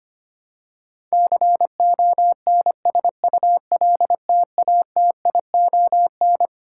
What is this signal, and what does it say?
Morse code, heard as a keyed tone, about 25 wpm, 700 Hz. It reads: CONSULTATION
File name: Twelve Letter Words - 25wpm